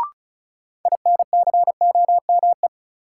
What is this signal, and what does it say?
Morse code, heard as a keyed tone, about 34 wpm, 700 Hz. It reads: INCOME
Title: Sentences from Top 1000 Words - Review Part 5 - New Word Drill - 35wpm